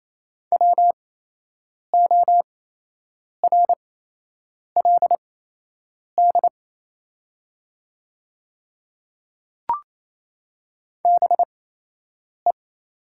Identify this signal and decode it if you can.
Morse code, heard as a keyed tone, about 28 wpm, spaced as for 8 wpm, 700 Hz. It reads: WORLD BE
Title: 8x Character Spacing - Top 200 Words - 28wpm